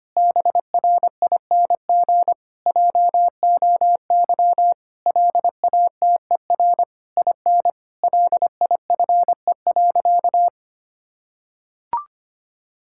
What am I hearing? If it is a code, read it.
Morse code, heard as a keyed tone, about 25 wpm, 700 Hz. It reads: BRING JOY LATER IN LIFE.
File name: practice / 25wpm / Difficult Sentences with Hints - 25wpm